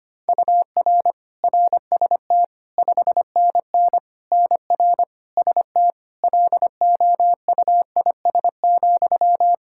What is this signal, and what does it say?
Morse code, heard as a keyed tone, about 25 wpm, 700 Hz. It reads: UR RST 5NN NR ST LOUIS,
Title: Practice QSO - Exchange 1 - Beginning - 25wpm